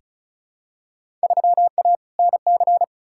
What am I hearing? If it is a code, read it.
Morse code, heard as a keyed tone, about 36 wpm, 700 Hz. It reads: 3A NC
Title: Field Day Exchanges - 35wpm